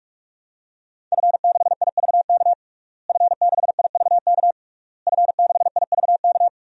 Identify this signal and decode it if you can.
Morse code, heard as a keyed tone, about 46 wpm, 700 Hz. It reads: F6IVK F6IVK F6IVK
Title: DXCC-100 Call Signs - 5 Characters - Repeated 3x - 45wpm